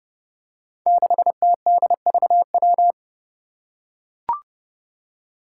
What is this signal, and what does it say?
Morse code, heard as a keyed tone, about 28 wpm, 700 Hz. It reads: 6TDVW
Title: Five Challenging Letter-Number Combinations 30wpm